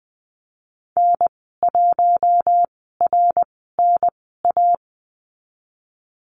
Morse code, heard as a keyed tone, about 20 wpm, spaced as for 15 wpm, 700 Hz. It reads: N1RNA
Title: 069 Lesson 12g - Letter d - Call Signs 20wpm